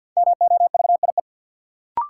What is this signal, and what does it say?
Morse code, heard as a keyed tone, about 51 wpm, 700 Hz. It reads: MOVIE
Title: Sentences from Top 1000 Words - Review Part 5 - New Word Drill - 50wpm